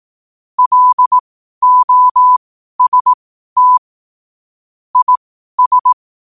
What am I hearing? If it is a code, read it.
Morse code, heard as a keyed tone, about 18 wpm, spaced as for 12 wpm, 1000 Hz. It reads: LOST IS
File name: 12wpm